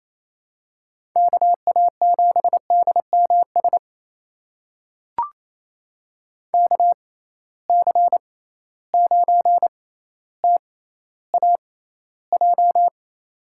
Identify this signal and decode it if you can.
Morse code, heard as a keyed tone, about 28 wpm, spaced as for 14 wpm, 700 Hz. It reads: KA7DMS KC9TAJ
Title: USA Call Signs - 6 Characters - 6x Character Spacing - 28wpm